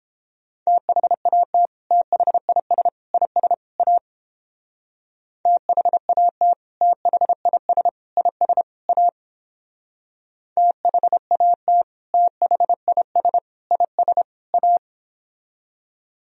Morse code, heard as a keyed tone, about 30 wpm, 700 Hz. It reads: THAT THIS IS A THAT THIS IS A THAT THIS IS A